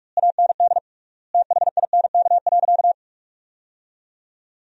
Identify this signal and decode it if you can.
Morse code, heard as a keyed tone, about 46 wpm, 700 Hz. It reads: AND THINK.
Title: 3x Word-Spacing - Sentences from Top 300 Words - 45wpm